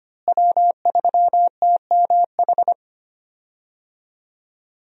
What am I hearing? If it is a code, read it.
Morse code, heard as a keyed tone, about 25 wpm, 700 Hz. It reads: W3TMH
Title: USA Call Signs - 5 Characters - Repeated 2x - 25wpm